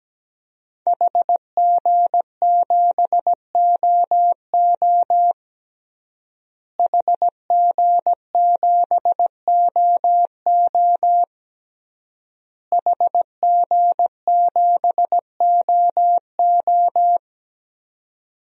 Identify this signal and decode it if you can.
Morse code, heard as a keyed tone, about 17 wpm, 700 Hz. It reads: HG7OO HG7OO HG7OO
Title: DXCC-100 Call Signs - 5 Characters - Repeated 3x - 17wpm